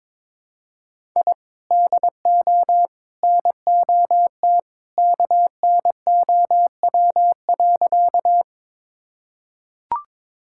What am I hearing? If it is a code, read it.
Morse code, heard as a keyed tone, about 22 wpm, 700 Hz. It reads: I DO NOT KNOW.
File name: practice / 22wpm / Sentences with 3-Word Phrases - 22wpm